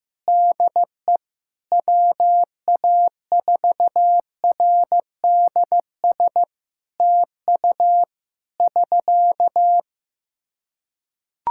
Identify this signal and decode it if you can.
Morse code, heard as a keyed tone, about 15 wpm, 700 Hz. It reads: DE WA4RDS TU <SK>
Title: Practice QSO - Exchange 4 - Ending - 15wpm